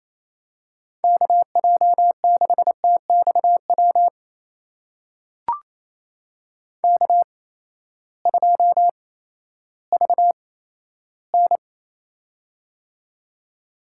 Morse code, heard as a keyed tone, about 28 wpm, spaced as for 14 wpm, 700 Hz. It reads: KJ6TXW K2VN
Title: USA Call Signs - 8x Character Spacing - 28wpm